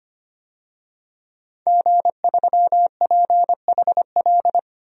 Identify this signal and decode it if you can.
Morse code, heard as a keyed tone, about 25 wpm, 700 Hz. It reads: G3PHL
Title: DXCC-100 Call Signs - 5 Characters - 25wpm